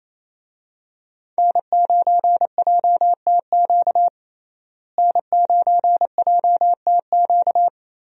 Morse code, heard as a keyed tone, about 28 wpm, 700 Hz. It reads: N9JTQ N9JTQ
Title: USA Call Signs - 5 Characters - Repeated 3x - 28wpm